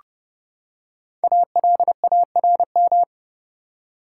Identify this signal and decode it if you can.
Morse code, heard as a keyed tone, about 30 wpm, 700 Hz. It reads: ALARM
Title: Five Letter Words - 30wpm